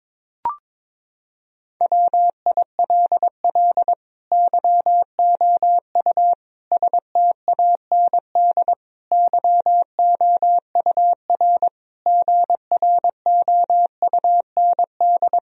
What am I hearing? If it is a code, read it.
Morse code, heard as a keyed tone, about 22 wpm, 700 Hz. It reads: WILL YOU STAND YOUR GROUND
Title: Sentences from Top 1000 Words - Review Part 2 - 22wpm